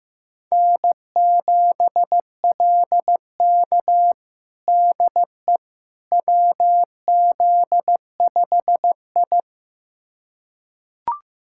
Morse code, heard as a keyed tone, about 16 wpm, 700 Hz. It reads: N7LK DE WZ5I
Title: USA Callsign DE Callsign - 4 Characters - 15wpm